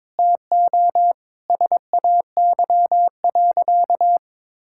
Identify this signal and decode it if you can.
Morse code, heard as a keyed tone, about 22 wpm, 700 Hz. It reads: TO SAY.
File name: practice / 22wpm / Sentences with 4-Word Phrases - 22wpm